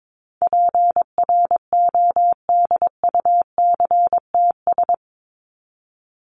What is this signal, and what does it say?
Morse code, heard as a keyed tone, about 22 wpm, 700 Hz. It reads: PRODUCTS
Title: ES:S Suffix - Top 100 Words with Base Words 22wpm